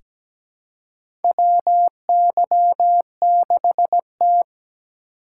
Morse code, heard as a keyed tone, about 17 wpm, 700 Hz. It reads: WY6T
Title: USA Call Signs - 4 Characters - 8x Character Spacing - 17wpm